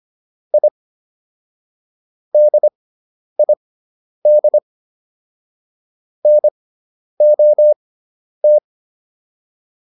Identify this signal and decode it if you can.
Morse code, heard as a keyed tone, about 25 wpm, spaced as for 10 wpm, 600 Hz. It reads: I DID NOT